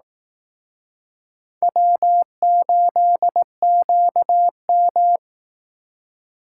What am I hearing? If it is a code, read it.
Morse code, heard as a keyed tone, about 20 wpm, 700 Hz. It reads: W8QM